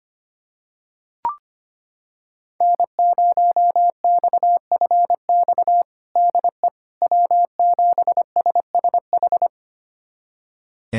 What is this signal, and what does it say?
Morse code, heard as a keyed tone, about 25 wpm, 700 Hz. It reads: N0XFX DE W7SSH
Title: USA Callsign DE Callsign - 5 Characters - 25wpm